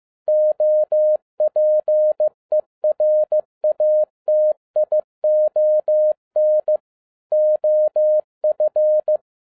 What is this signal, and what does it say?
Morse code, heard as a keyed tone, about 15 wpm, 610 Hz. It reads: OPERATION OF